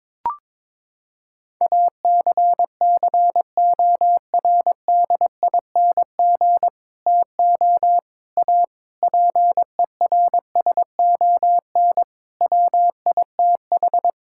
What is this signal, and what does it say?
Morse code, heard as a keyed tone, about 22 wpm, 700 Hz. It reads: ACCORDING TO A PERSON WITH
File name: practice / 22wpm / Sets of 8 Words - 22wpm